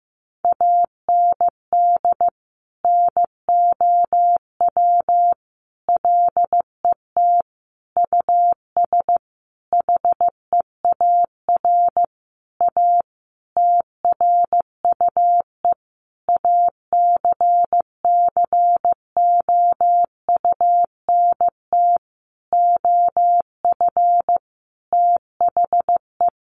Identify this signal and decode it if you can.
Morse code, heard as a keyed tone, about 15 wpm, 700 Hz. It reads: AND NOW LET US HEAR A TRUE ACCOUNT OF THE
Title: Sentences from Top 900 Words 15wpm